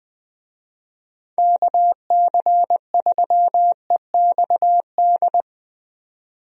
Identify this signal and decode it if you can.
Morse code, heard as a keyed tone, about 20 wpm, 700 Hz. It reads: KC3EXD
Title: USA Call Signs - 6 Characters - 20wpm